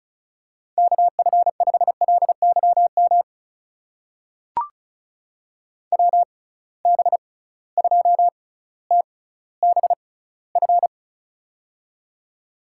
Morse code, heard as a keyed tone, about 36 wpm, spaced as for 18 wpm, 700 Hz. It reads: KF5LYM WB2TBF
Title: USA Call Signs - 6 Characters - 6x Character Spacing - 35wpm